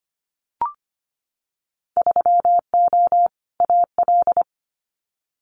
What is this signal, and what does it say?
Morse code, heard as a keyed tone, about 25 wpm, 690 Hz. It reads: 3O AL
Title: Winter Field Day Exchanges 25wpm